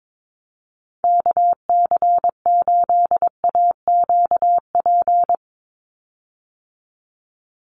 Morse code, heard as a keyed tone, about 22 wpm, 700 Hz. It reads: KC8AQP
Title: USA Call Signs 22wpm